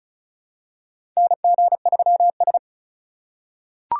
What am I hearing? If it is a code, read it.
Morse code, heard as a keyed tone, about 34 wpm, 700 Hz. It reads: NG3S
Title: USA Call Signs - 2x Character Spacing - 35wpm